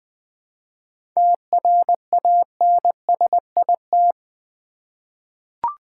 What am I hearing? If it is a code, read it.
Morse code, heard as a keyed tone, about 20 wpm, 700 Hz. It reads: TRANSIT
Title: 7 Letter Words - First 3 Letters Given - 20wpm